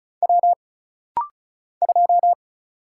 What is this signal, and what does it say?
Morse code, heard as a keyed tone, about 35 wpm, 700 Hz. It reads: W 2
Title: Single Letter-Number - Rapid-Fire 35wpm